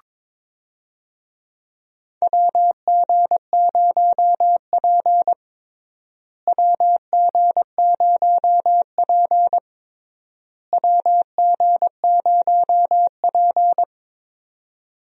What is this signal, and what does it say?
Morse code, heard as a keyed tone, about 22 wpm, 700 Hz. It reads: WG0P WG0P WG0P
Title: USA Call Signs - 4 Characters - Repeated 3x - 22wpm